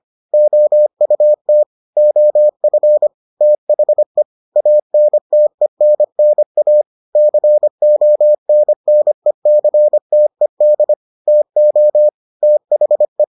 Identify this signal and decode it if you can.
Morse code, heard as a keyed tone, about 25 wpm, 600 Hz. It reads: OUT OF THE ANTENNA CONNECTED TO THE